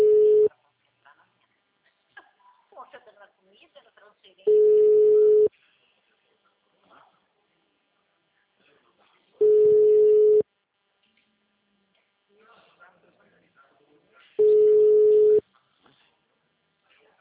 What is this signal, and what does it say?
Ringback tone (Europe, 425 Hz), with speech.